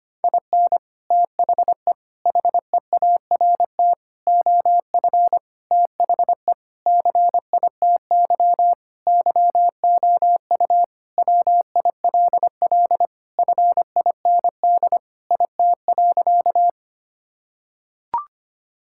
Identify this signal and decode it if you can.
Morse code, heard as a keyed tone, about 25 wpm, 700 Hz. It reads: IN THE HEART OF THE CITY YOU WILL FIND IT.
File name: practice / 25wpm / Sentences with 3-Word Phrases - 25wpm